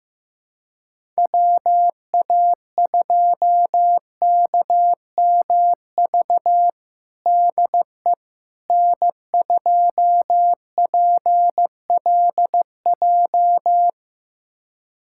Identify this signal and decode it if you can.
Morse code, heard as a keyed tone, about 15 wpm, 700 Hz. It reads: WA2KMV DE N2PLJ
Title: USA Callsign DE Callsign 15wpm